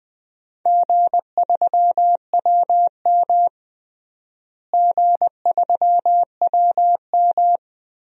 Morse code, heard as a keyed tone, about 20 wpm, 700 Hz. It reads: G3WM G3WM
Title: DXCC-100 Call Signs - 4 Characters - Repeated 2x - 20wpm